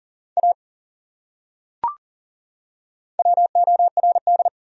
Morse code, heard as a keyed tone, about 39 wpm, 700 Hz. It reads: A WORD
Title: Top 100 Words 40wpm